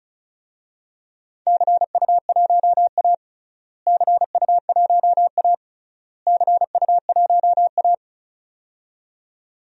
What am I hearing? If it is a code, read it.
Morse code, heard as a keyed tone, about 36 wpm, 690 Hz. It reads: CU1A CU1A CU1A
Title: DXCC-100 Call Signs - 4 Characters - Repeated 3x - 35wpm